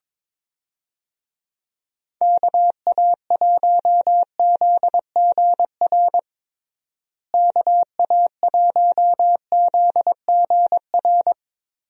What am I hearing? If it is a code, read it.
Morse code, heard as a keyed tone, about 22 wpm, 700 Hz. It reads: KA1ZGR KA1ZGR
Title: USA Call Signs - 6 Characters - Repeated 3x - 22wpm